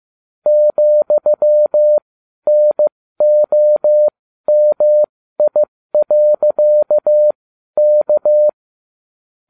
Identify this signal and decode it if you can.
Morse code, heard as a keyed tone, about 15 wpm, spaced as for 12 wpm, 600 Hz. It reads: ,NOMI.K
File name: cw / ubacw20 2